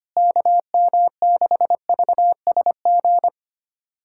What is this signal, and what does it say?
Morse code, heard as a keyed tone, about 25 wpm, 700 Hz. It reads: KM6VSG
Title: USA Callsign DE Callsign 25wpm